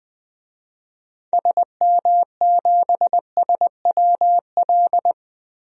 Morse code, heard as a keyed tone, about 21 wpm, 700 Hz. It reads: SM7SWL
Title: DXCC-100 Call Signs - 6 Characters - Repeated 2x - 20wpm